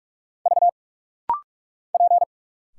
Morse code, heard as a keyed tone, about 46 wpm, 700 Hz. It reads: V P
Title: Single Letters - Rapid-Fire 45wpm